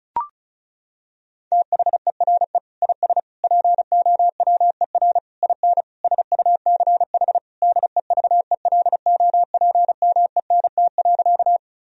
Morse code, heard as a keyed tone, about 36 wpm, 700 Hz. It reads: THERE IS POWER IN SUCH DEVELOPMENT.
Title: Sentences from Top 1000 Words - Review Part 4 - 35wpm